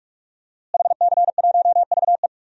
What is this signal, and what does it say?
Morse code, heard as a keyed tone, about 46 wpm, 700 Hz. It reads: HC1VE
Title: DXCC-100 Call Signs - 5 Characters - Repeated 2x - 45wpm